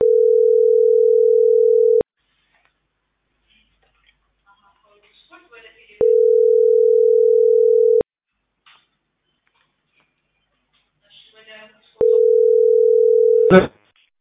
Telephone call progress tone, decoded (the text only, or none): ringback tone